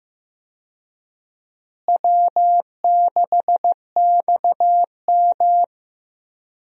Morse code, heard as a keyed tone, about 15 wpm, 700 Hz. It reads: W6XM
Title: USA Call Signs - 4 Characters - Repeated 2x - 15wpm